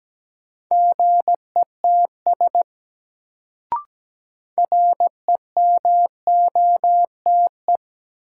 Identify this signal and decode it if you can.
Morse code, heard as a keyed tone, about 17 wpm, 700 Hz. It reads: GETS REMOTE
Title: Sentences from Top 1300 Words - New Word Drill - 17wpm